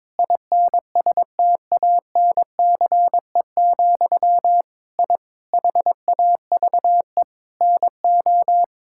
Morse code, heard as a keyed tone, about 22 wpm, 700 Hz. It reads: INSTANCE, I HAVE NO
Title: Sentences from Top 1000 Words - Review Part 5 - 22wpm